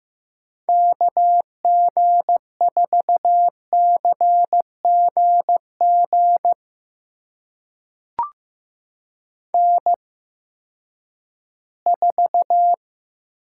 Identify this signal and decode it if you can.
Morse code, heard as a keyed tone, about 15 wpm, spaced as for 10 wpm, 700 Hz. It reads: KG4CGG N4
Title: USA Call Signs - 8x Character Spacing - 15wpm